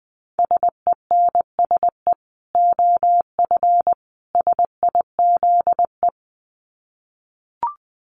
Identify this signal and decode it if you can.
Morse code, heard as a keyed tone, about 20 wpm, 700 Hz. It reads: SENSE OF SIZE
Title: Dit Challenge 20wpm